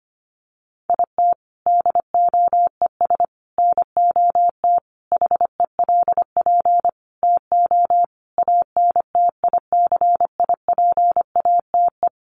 Morse code, heard as a keyed tone, about 25 wpm, 700 Hz. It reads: IT DOES NOT HELP TO ANTICIPATE
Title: Sentences from Top 1500 Words 25wpm